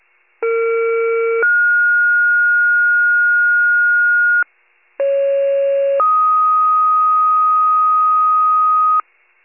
Two-tone page, heard as a one-second tone, then a three-second tone, about 460 then 1430 Hz, about 550 then 1250 Hz, with noise. It is heard from 0.4 to 9.0 s.